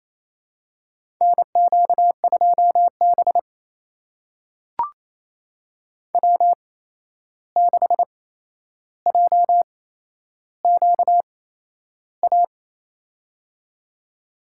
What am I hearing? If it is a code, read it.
Morse code, heard as a keyed tone, about 28 wpm, spaced as for 11 wpm, 700 Hz. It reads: NQ2B W6JQA
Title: USA Call Signs - 8x Character Spacing - 28wpm